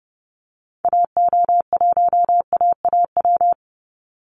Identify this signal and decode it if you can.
Morse code, heard as a keyed tone, about 30 wpm, 700 Hz. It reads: AO1AAW
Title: Call Signs Repeated 3x - 30wpm